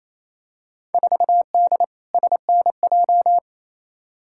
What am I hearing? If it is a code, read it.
Morse code, heard as a keyed tone, about 28 wpm, 700 Hz. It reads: 4D SNJ
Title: Field Day Exchanges Encore - 28wpm